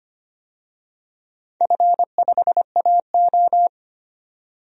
Morse code, heard as a keyed tone, about 25 wpm, 700 Hz. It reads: F5AO
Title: DXCC-100 Call Signs - 4 Characters - 25wpm